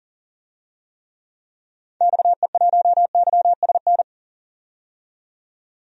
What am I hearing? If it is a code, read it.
Morse code, heard as a keyed tone, about 40 wpm, 700 Hz. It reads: XE1YSN
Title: DXCC-100 Call Signs - 6 Characters - 40wpm